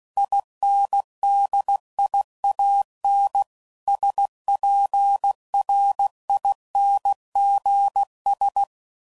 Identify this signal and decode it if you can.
Morse code, heard as a keyed tone, about 16 wpm, 800 Hz. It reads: INDIAN SPRINGS
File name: CW Code Groups @15WPM